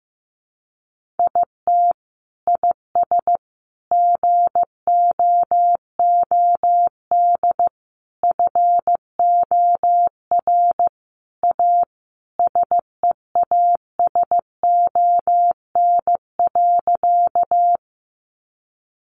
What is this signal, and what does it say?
Morse code, heard as a keyed tone, about 15 wpm, 700 Hz. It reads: IT IS GOOD FOR A SEASON.
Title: Sentences from Top 1400 Words 15wpm